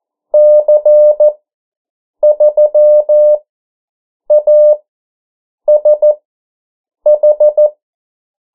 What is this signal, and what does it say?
Morse code, heard as a keyed tone, about 14 wpm, spaced as for 7 wpm, 600 Hz. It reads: C3ASH
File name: morse mp3 / 09 urok 8